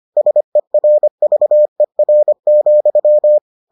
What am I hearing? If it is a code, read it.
Morse code, heard as a keyed tone, about 25 wpm, 600 Hz. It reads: SERVER,